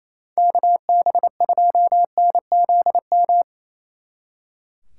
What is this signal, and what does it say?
Morse code, heard as a keyed tone, about 35 wpm, 700 Hz. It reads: KB2NZM